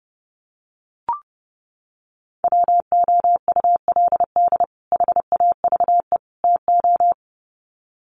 Morse code, heard as a keyed tone, about 30 wpm, 700 Hz. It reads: WOULD HAVE TO
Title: Sets of 3 Words 30wpm